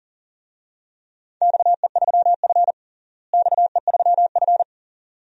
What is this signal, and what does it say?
Morse code, heard as a keyed tone, about 40 wpm, 700 Hz. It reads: XE3F XE3F
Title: DXCC-100 Call Signs - 4 Characters - Repeated 2x - 40wpm